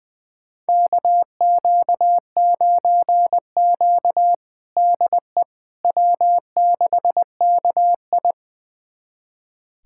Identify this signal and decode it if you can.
Morse code, heard as a keyed tone, about 20 wpm, 700 Hz. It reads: KQ9Q DE W6KI
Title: USA Callsign DE Callsign - 4 Characters - 20wpm